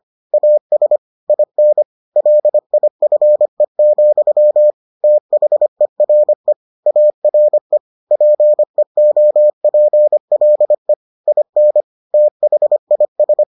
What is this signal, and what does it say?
Morse code, heard as a keyed tone, about 25 wpm, 600 Hz. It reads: AS IN LIFE, THERE ARE PEOPLE IN THIS